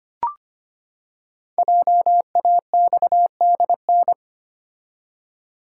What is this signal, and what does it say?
Morse code, heard as a keyed tone, about 25 wpm, 700 Hz. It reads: JAXDN
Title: Code Groups - 5 Letters - 25wpm